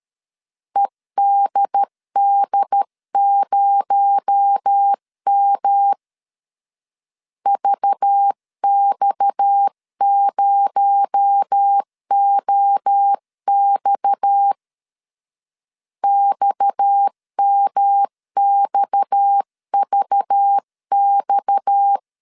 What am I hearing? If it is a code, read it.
Morse code, heard as a keyed tone, about 13 wpm, 800 Hz. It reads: EDD0M VX0OX XMXVX
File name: CW / DET / 1 6 1500C S C 12wpm 3533